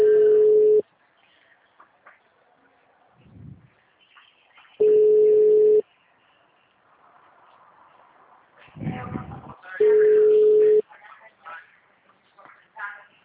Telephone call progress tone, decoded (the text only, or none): ringback tone